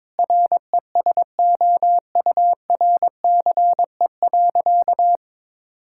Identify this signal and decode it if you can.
Morse code, heard as a keyed tone, about 22 wpm, 700 Hz. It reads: RESOURCE.
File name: Difficult Sentences with Hints - 22wpm